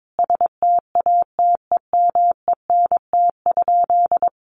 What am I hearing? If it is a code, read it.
Morse code, heard as a keyed tone, about 22 wpm, 700 Hz. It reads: STATEMENT?
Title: Sentences from Top 1200 Words 22wpm